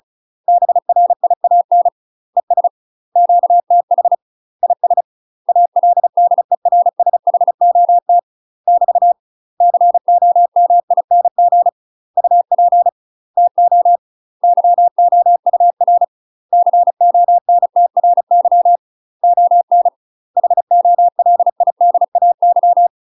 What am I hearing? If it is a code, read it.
Morse code, heard as a keyed tone, about 36 wpm, 700 Hz. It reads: BRIAN ES QTH IS ALDERSHOT <BT> COMING UP TO YOUR CONTRY ON HOLIDAY
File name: UK MASTER QSOs TRAINING / 35WPM / UK MASTER QSOs 0036